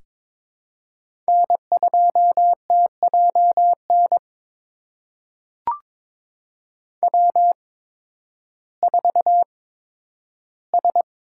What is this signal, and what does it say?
Morse code, heard as a keyed tone, about 22 wpm, spaced as for 12 wpm, 700 Hz. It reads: N2TJN W4S
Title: USA Call Signs - 5 Characters - 8x Character Spacing - 22wpm